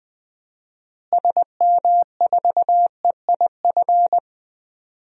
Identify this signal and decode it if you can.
Morse code, heard as a keyed tone, about 19 wpm, 700 Hz. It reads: SM4EIF
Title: DXCC-100 Call Signs - 6 Characters - Repeated 3x - 20wpm